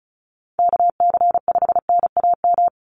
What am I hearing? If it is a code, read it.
Morse code, heard as a keyed tone, about 34 wpm, 700 Hz. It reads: KC5NAM
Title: USA Call Signs 35wpm